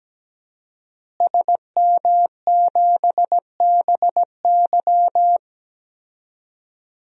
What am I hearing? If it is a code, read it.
Morse code, heard as a keyed tone, about 17 wpm, 700 Hz. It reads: SM7BY
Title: DXCC-100 Call Signs - 5 Characters - Repeated 2x - 17wpm